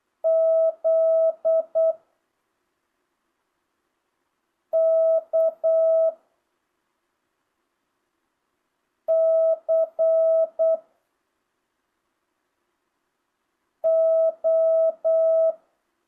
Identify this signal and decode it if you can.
Morse code, heard as a keyed tone, about 8 wpm, 660 Hz. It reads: Z K C O
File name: jh8cbh / 2025cw